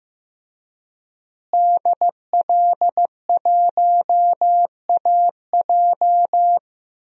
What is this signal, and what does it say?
Morse code, heard as a keyed tone, about 15 wpm, 700 Hz. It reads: DL1AJ
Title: DXCC-100 Call Signs - 5 Characters - Repeated 3x - 15wpm